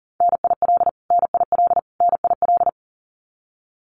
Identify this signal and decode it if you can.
Morse code, heard as a keyed tone, about 40 wpm, 700 Hz. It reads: NIL NIL NIL
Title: Abbreviations Repeated 3x - 40wpm